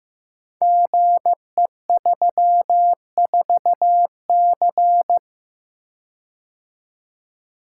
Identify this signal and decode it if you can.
Morse code, heard as a keyed tone, about 15 wpm, 700 Hz. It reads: GE34C
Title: Code Groups - 5 Letter:Numbers - 15wpm